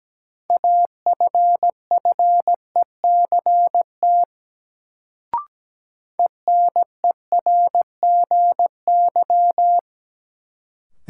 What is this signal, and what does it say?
Morse code, heard as a keyed tone, about 17 wpm, 700 Hz. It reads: AFFECT ENERGY
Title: Sentences from Top 900 Words - New Word Drill - 17wpm